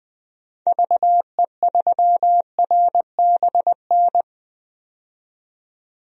Morse code, heard as a keyed tone, about 20 wpm, 700 Hz. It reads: VE3RBN
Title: DXCC-100 Call Signs - 6 Characters - 20wpm